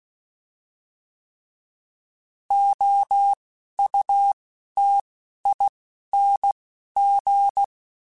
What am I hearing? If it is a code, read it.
Morse code, heard as a keyed tone, about 16 wpm, spaced as for 11 wpm, 800 Hz. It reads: OUTING
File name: Alphabetical / Alphabetical mp3s / Os @10